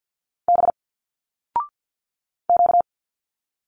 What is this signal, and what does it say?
Morse code, heard as a keyed tone, about 50 wpm, 700 Hz. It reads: B Q
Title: Single Letters 50wpm